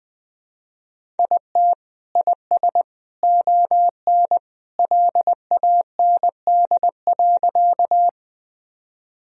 Morse code, heard as a keyed tone, about 21 wpm, 700 Hz. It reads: IT IS ON LAND.
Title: Sentences from Top 1000 Words - Review Part 1 - 20wpm